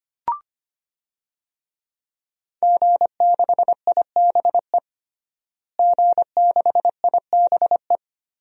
Morse code, heard as a keyed tone, about 25 wpm, 700 Hz. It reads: G6IBE G6IBE
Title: DXCC-100 Call Signs - 5 Characters - Repeated 2x - 25wpm